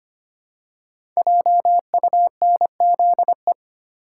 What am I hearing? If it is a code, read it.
Morse code, heard as a keyed tone, about 25 wpm, 700 Hz. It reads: JUNZE
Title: Code Groups - 5 Letters - 25wpm